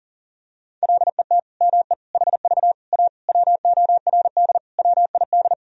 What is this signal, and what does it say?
Morse code, heard as a keyed tone, about 40 wpm, 700 Hz. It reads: LET ME HV A WORD WID
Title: Sentences from Top 300 Words and Abbreviations 40wpm